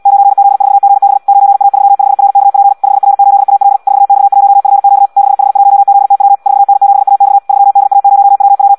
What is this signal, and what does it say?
Morse code, heard as a keyed tone, about 74 wpm, 800 Hz. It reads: 9Z4GB 9A35RKP 5R8AL 4X1FQ 6H1ZTW 4N7AX 3DA0WW
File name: Audio practica CW / qrq-tx-74 2